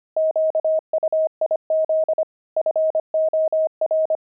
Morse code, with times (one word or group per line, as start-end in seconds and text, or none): 0.2-2.2 QUIZ
2.6-4.2 FOR